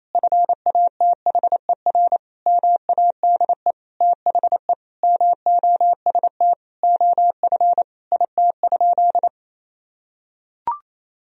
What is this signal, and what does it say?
Morse code, heard as a keyed tone, about 28 wpm, 700 Hz. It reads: FATHER MADE THE MOST OF IT?
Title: Sentences from Top 1000 Words - Review Part 1 - 28wpm